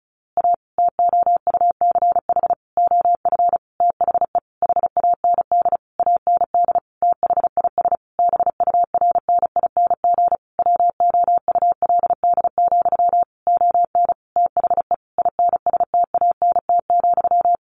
Morse code, heard as a keyed tone, about 34 wpm, 700 Hz. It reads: A TOUCH OF THE HAND AND THIS BURNING WOULD, ON THE INSTANT,
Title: A Sound Of Thunder 35wpm